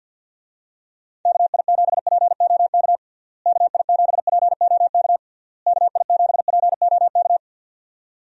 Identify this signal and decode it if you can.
Morse code, heard as a keyed tone, about 50 wpm, 700 Hz. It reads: KI7POK KI7POK KI7POK
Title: USA Call Signs - 6 Characters - Repeated 3x - 50wpm